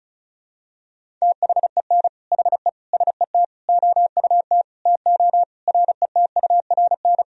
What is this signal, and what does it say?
Morse code, heard as a keyed tone, about 34 wpm, 700 Hz. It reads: THEN HE SET OUT TO RETURN
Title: Sentences from Top 1000 Words - Review Part 1 - 35wpm